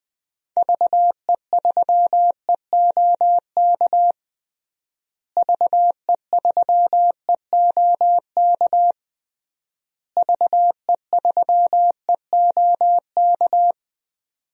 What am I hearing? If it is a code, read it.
Morse code, heard as a keyed tone, about 20 wpm, 700 Hz. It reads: VE3EOK VE3EOK VE3EOK
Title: DXCC-100 Call Signs - 6 Characters - Repeated 3x - 20wpm